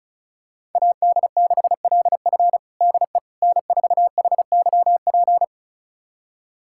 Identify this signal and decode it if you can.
Morse code, heard as a keyed tone, about 36 wpm, 700 Hz. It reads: AD6LF DE N4HYP
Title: USA Callsign DE Callsign - 5 Characters - 35wpm